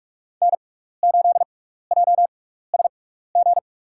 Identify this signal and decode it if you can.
Morse code, heard as a keyed tone, about 42 wpm, spaced as for 15 wpm, 700 Hz. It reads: N8JSG
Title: USA Call Signs - 5 Characters - 6x Character Spacing - 45wpm